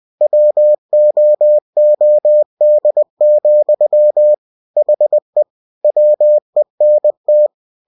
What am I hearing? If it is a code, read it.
Morse code, heard as a keyed tone, about 20 wpm, 600 Hz. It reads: WOOD, HE WENT
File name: cw-books / winnie-the-pooh / winnie-20wpm-page-0050